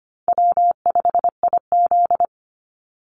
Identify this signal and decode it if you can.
Morse code, heard as a keyed tone, about 25 wpm, 700 Hz. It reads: W5IZ
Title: Call Signs - 25wpm